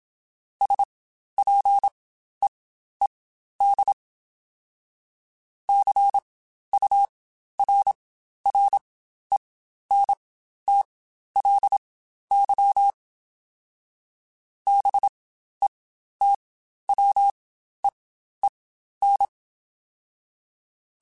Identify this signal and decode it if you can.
Morse code, heard as a keyed tone, about 27 wpm, spaced as for 12 wpm, 800 Hz. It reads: SPEED CURRENTLY BETWEEN
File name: Spaced Characters / Spaced Characters-mp3s / Weather 5 22 sp @25